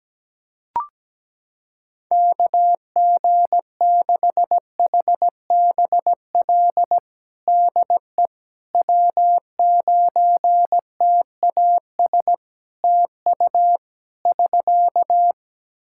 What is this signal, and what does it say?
Morse code, heard as a keyed tone, about 17 wpm, 700 Hz. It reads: KG6HBL DE W9TAS TU <SK>
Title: Practice QSO - Exchange 4 - Ending - 17wpm